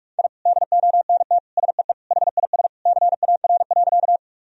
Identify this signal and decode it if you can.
Morse code, heard as a keyed tone, about 46 wpm, 700 Hz. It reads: I DONT SEE HIS CAR.
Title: Sentences from Top 1000 Words - Review Part 1 - 45wpm